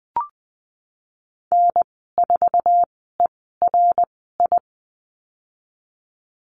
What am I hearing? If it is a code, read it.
Morse code, heard as a keyed tone, about 20 wpm, spaced as for 15 wpm, 700 Hz. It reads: N4ERI
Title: 062 Lesson 11g - Letter h - Call Signs 20wpm